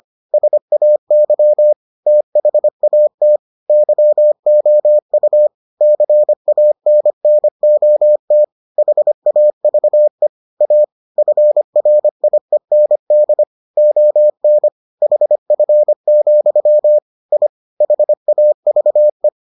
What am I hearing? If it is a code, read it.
Morse code, heard as a keyed tone, about 25 wpm, 600 Hz. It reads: SAY THAT YOU CANNOT HAVE A FRIEND ON HF, I HAVE